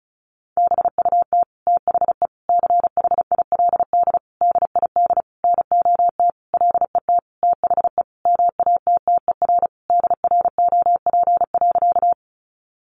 Morse code, heard as a keyed tone, about 34 wpm, 700 Hz. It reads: BUT THE CHILD DID NOT LET THE MATTER DROP.
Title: Sentences from Top 1400 Words 35wpm